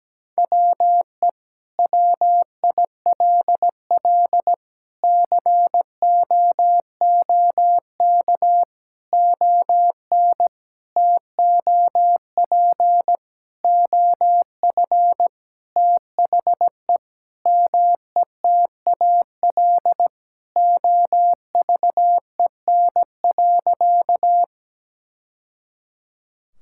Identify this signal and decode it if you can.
Morse code, heard as a keyed tone, about 17 wpm, 700 Hz. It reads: WE WILL COOK ON TOP OF THE METAL OVEN.
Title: Sentences from Top 1000 Words - Review Part 6 - 17wpm